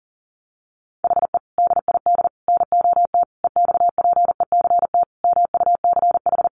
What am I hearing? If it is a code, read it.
Morse code, heard as a keyed tone, about 40 wpm, 700 Hz. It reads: HE DID NOT EXPECT MUCH
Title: Sentences from Top 1100 Words 40wpm